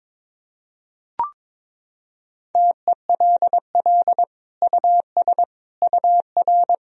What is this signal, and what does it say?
Morse code, heard as a keyed tone, about 21 wpm, 700 Hz. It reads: TELL US UR
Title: Sentences from Top 200 Words and Abbreviations 22wpm